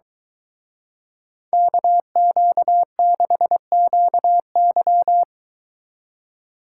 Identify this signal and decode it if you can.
Morse code, heard as a keyed tone, about 22 wpm, 700 Hz. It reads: KQ6QY